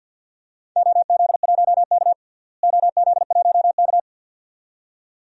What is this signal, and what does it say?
Morse code, heard as a keyed tone, about 51 wpm, 700 Hz. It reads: OZ1K OZ1K
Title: DXCC-100 Call Signs - 4 Characters - Repeated 2x - 50wpm